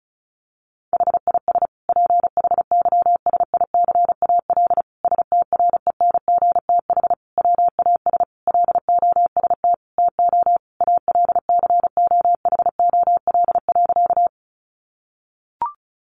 Morse code, heard as a keyed tone, about 34 wpm, 700 Hz. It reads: HIS PHYSICAL STRENGTH WAS LOST TO ALCOHOL.
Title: Sentences from Top 1400 Words 35wpm